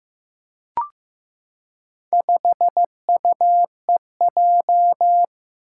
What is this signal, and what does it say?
Morse code, heard as a keyed tone, about 15 wpm, 700 Hz. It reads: 5UEJ
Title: Four Challenging Letter-Number Combinations 15wpm